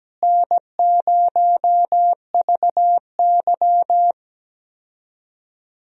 Morse code, heard as a keyed tone, about 17 wpm, 700 Hz. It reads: N0VY
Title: USA Call Signs - 4 Characters - Repeated 2x - 17wpm